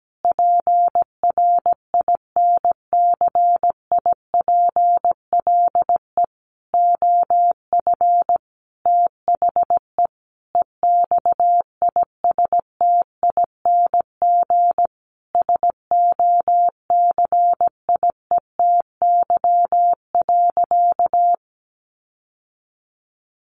Morse code, heard as a keyed tone, about 17 wpm, 700 Hz. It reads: PRINCIPLE OF THE EXISTING SOCIETY.
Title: Sentences from Top 1500 Words 17wpm